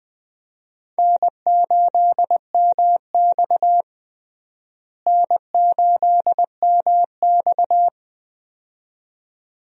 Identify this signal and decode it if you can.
Morse code, heard as a keyed tone, about 21 wpm, 700 Hz. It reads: N8MX N8MX
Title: USA Call Signs - 4 Characters - Repeated 2x - 20wpm